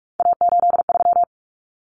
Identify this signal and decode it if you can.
Morse code, heard as a keyed tone, about 44 wpm, 700 Hz. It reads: A83
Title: Common Cut Numbers - 3-Digit 45wpm